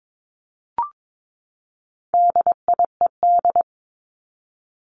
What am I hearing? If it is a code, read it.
Morse code, heard as a keyed tone, about 22 wpm, 700 Hz. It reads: DIED
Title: ED Suffix - Top 100 Words with Base Words 22wpm